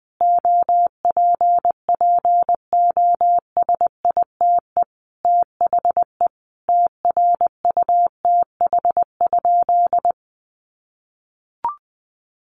Morse code, heard as a keyed tone, about 20 wpm, 700 Hz. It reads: OPPOSITE THE TRUTH?
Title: Sentences from Top 1300 Words 20wpm